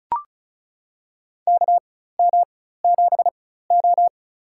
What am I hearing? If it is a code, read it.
Morse code, heard as a keyed tone, about 36 wpm, spaced as for 16 wpm, 700 Hz. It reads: KM7O
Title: USA Call Signs - 4 Characters - 4x Character Spacing - 35wpm